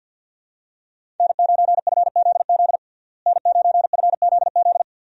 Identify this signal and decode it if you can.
Morse code, heard as a keyed tone, about 52 wpm, 700 Hz. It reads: N9FZZ N9FZZ
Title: USA Call Signs - 5 Characters - Repeated 3x - 50wpm